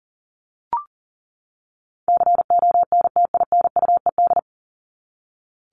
Morse code, heard as a keyed tone, about 40 wpm, 700 Hz. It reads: CONTINUED
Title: ED Suffix - Top 100 Words with Base Words 40wpm